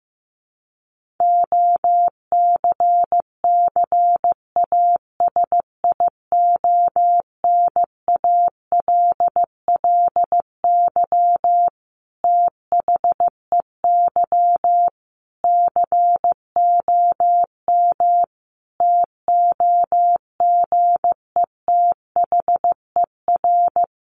Morse code, heard as a keyed tone, about 15 wpm, 700 Hz. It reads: OCCASIONALLY THEY COM TOGETHER
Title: Sentences from Top 1300 Words 15wpm